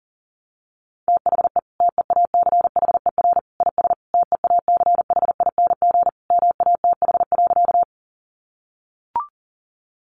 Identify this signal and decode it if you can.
Morse code, heard as a keyed tone, about 40 wpm, 700 Hz. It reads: THE TEACHER IS TEACHING MATH.
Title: Easy Phrases 40wpm